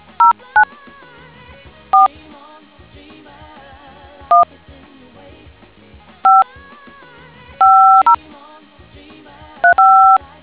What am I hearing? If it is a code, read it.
DTMF, heard as telephone touch tones, tones of about 150 ms. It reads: 09 4 1 5 50 35